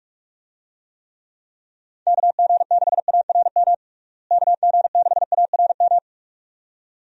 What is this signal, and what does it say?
Morse code, heard as a keyed tone, about 45 wpm, 700 Hz. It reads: KG6ARM KG6ARM
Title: USA Call Signs - 6 Characters - Repeated 2x - 45wpm